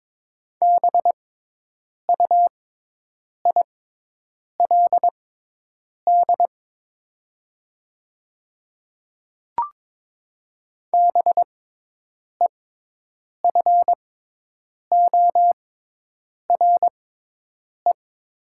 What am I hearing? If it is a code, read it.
Morse code, heard as a keyed tone, about 22 wpm, spaced as for 8 wpm, 700 Hz. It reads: BUILD BEFORE
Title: 6x Character Spacing - Top 300 Words - 22wpm